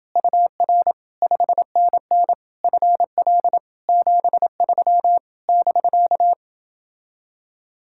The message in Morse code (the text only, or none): UR 5NN FL 73 <BK>